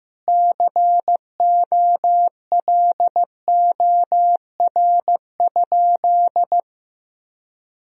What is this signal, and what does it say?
Morse code, heard as a keyed tone, about 16 wpm, 700 Hz. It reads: COLOR?
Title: Sentences from Top 1000 Words - Review Part 2 - 15wpm